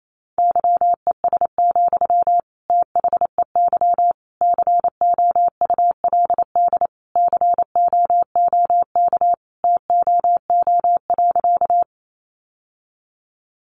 Morse code, heard as a keyed tone, about 28 wpm, 700 Hz. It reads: YES, THEY COULD COOK TOO.
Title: Sentences from Top 700 Words 28wpm